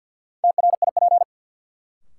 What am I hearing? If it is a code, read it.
Morse code, heard as a keyed tone, about 53 wpm, 700 Hz. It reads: TRIP
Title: Sentences from Top 1000 Words - Review Part 4 - New Word Drill - 50wpm